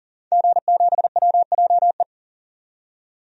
Morse code, heard as a keyed tone, about 39 wpm, 700 Hz. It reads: G7WJE
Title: DXCC-100 Call Signs - 5 Characters - 40wpm